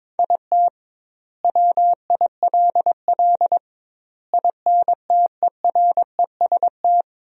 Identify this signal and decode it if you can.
Morse code, heard as a keyed tone, about 21 wpm, 700 Hz. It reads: IT WILL INTEREST
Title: 2x Word-Spacing - Sentences from Top 400 Words - 22wpm